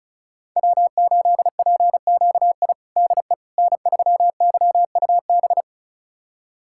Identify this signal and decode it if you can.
Morse code, heard as a keyed tone, about 36 wpm, 690 Hz. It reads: W8PQI DE N3YUB
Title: USA Callsign DE Callsign - 5 Characters - 35wpm